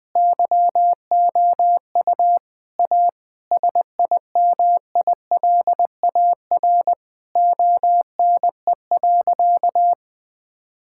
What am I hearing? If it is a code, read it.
Morse code, heard as a keyed tone, about 20 wpm, 700 Hz. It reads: YOU A SIMILAR ONE.
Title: Sentences from Top 1000 Words - Review Part 5 - 20wpm